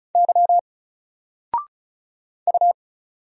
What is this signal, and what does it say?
Morse code, heard as a keyed tone, about 36 wpm, 700 Hz. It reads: Y U
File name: Single-Letter-Number-35wpm